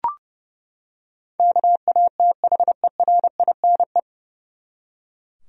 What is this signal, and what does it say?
Morse code, heard as a keyed tone, about 29 wpm, 700 Hz. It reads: KATHERINE
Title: USA Common Names - 30wpm